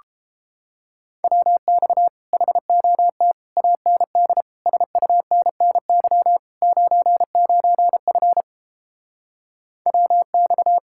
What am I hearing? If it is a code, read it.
Morse code, heard as a keyed tone, about 31 wpm, 700 Hz. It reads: WX HOT AND SUNNY 99F WX